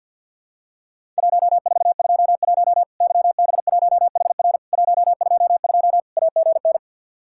Morse code, heard as a keyed tone, about 50 wpm, 700 Hz. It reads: 1421 YB1HR 122 AGN
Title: Worked All Europe DX Contest - QTC Exchange with Repeats - 50wpm